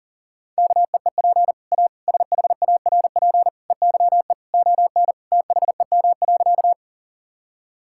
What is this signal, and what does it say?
Morse code, heard as a keyed tone, about 37 wpm, 700 Hz. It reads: KEEP A SHARP EYE ON THEM.
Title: Sentences from Top 1600 Words 40wpm